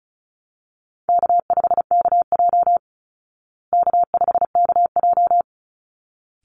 Morse code, heard as a keyed tone, about 34 wpm, 700 Hz. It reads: K5KJ K5KJ
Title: Easy Call Signs Repeated 2x - 35wpm